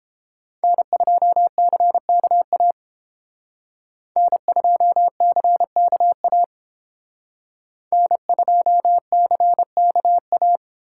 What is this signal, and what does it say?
Morse code, heard as a keyed tone, about 30 wpm, 700 Hz. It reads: N2CKA N2CKA N2CKA